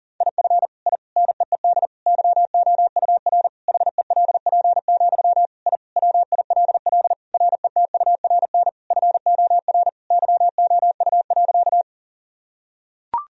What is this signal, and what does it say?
Morse code, heard as a keyed tone, about 40 wpm, 700 Hz. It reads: IF I NEED YOUR HELP, I WILL RETURN FOR YOU.
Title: Sentences from Top 1000 Words - Review Part 1 - 40wpm